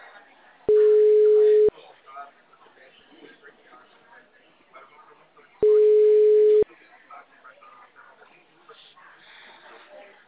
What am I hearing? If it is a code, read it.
Ringback tone (Europe, 425 Hz), with speech.